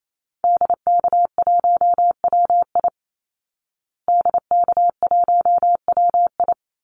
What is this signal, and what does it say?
Morse code, heard as a keyed tone, about 28 wpm, 700 Hz. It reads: DK1WI DK1WI
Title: Call Signs Repeated 2x - 28wpm